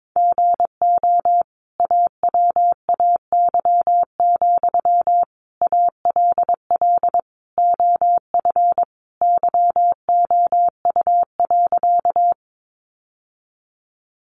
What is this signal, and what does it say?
Morse code, heard as a keyed tone, about 22 wpm, 700 Hz. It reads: GO AWAY, ALL OF YOU.
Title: Sentences from Top 600 Words 22wpm